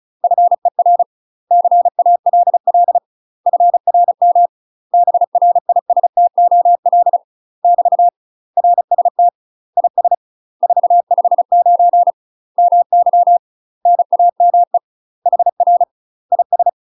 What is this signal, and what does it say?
Morse code, heard as a keyed tone, about 34 wpm, 700 Hz. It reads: FER CALL FRM BRISTOL <BT> RST IS 459 MY NAME HR IS
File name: UK MASTER QSOs TRAINING / 35WPM / UK MASTER QSOs 0034